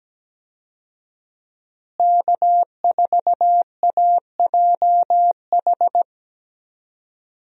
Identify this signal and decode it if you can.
Morse code, heard as a keyed tone, about 17 wpm, 700 Hz. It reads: K4AJH
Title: USA Call Signs - 5 Characters - Repeated 2x - 17wpm